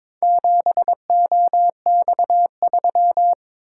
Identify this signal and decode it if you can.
Morse code, heard as a keyed tone, about 22 wpm, 700 Hz. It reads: 7OX3
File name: practice / 22wpm / Four Letter-Number Combinations - 22wpm